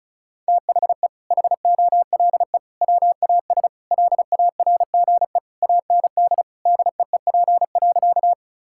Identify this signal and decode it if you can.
Morse code, heard as a keyed tone, about 36 wpm, 700 Hz. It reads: THE HOLE WAS LARGE AND DEEP.
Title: Sentences from Top 1600 Words 35wpm